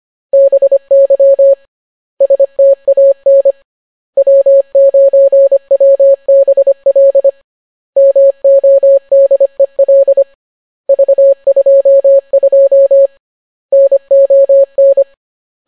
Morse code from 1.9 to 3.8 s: STAN